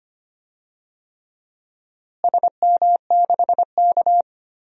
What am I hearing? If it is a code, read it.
Morse code, heard as a keyed tone, about 25 wpm, 700 Hz. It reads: SM6K
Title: DXCC-100 Call Signs - 4 Characters - Repeated 3x - 25wpm